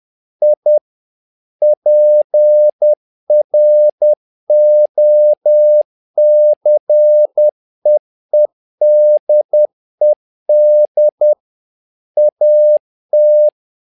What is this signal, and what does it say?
Morse code, heard as a keyed tone, about 10 wpm, 600 Hz. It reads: I PROCEEDED AT
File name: learning center / tower trouble / tower trouble 10wpm